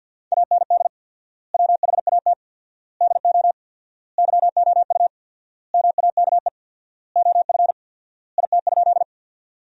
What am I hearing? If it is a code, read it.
Morse code, heard as a keyed tone, about 50 wpm, 700 Hz. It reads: AND WHAT DO YOU MAKE OF IT?
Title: 4x Word-Spacing - Sentences from Top 100 Words - 50wpm